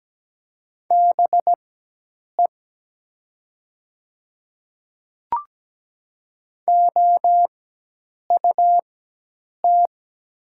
Morse code, heard as a keyed tone, about 17 wpm, spaced as for 8 wpm, 700 Hz. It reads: BE OUT
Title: 4x Character Spacing - Top 100 Words - 17wpm